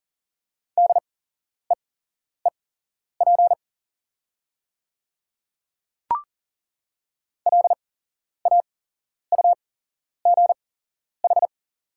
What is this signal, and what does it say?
Morse code, heard as a keyed tone, about 40 wpm, spaced as for 11 wpm, 690 Hz. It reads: DEEP LAUGH
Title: 8x Character Spacing - Top 500 Words - 40wpm